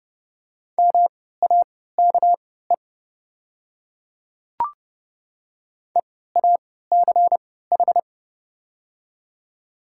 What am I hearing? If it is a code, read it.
Morse code, heard as a keyed tone, about 30 wpm, spaced as for 17 wpm, 700 Hz. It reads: MAKE EACH
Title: 3x Character Spacing - Top 100 Words - 30wpm